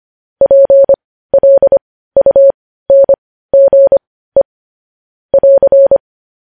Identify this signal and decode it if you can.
Morse code, heard as a keyed tone, about 25 wpm, spaced as for 15 wpm, 550 Hz. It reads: PLUNGE <AR>
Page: News Headlines in Morse Code at 15 WPM